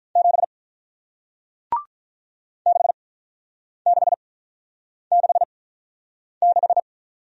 Morse code, heard as a keyed tone, about 48 wpm, spaced as for 9 wpm, 700 Hz. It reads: 7 6666